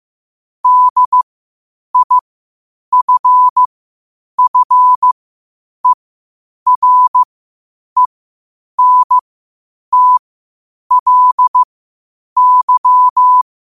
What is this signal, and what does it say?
Morse code, heard as a keyed tone, about 15 wpm, spaced as for 8 wpm, 1000 Hz. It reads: DIFFERENTLY
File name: qotdfiles / 8wpm